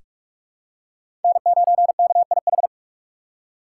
Morse code, heard as a keyed tone, about 46 wpm, 700 Hz. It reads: N9KIH
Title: USA Call Signs - 5 Characters - 6x Character Spacing - 45wpm